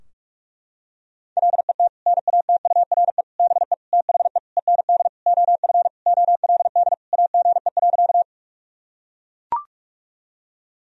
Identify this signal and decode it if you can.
Morse code, heard as a keyed tone, about 41 wpm, 700 Hz. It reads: LET NATURE BE THE END OF OLD AGE.
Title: Sentences from Top 1000 Words - Review Part 3 - 45wpm